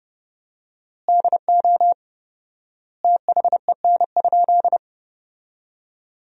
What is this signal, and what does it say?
Morse code, heard as a keyed tone, about 30 wpm, 700 Hz. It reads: DO THEN?
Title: 4x Word-Spacing - Sentences from Top 100 Words - 30wpm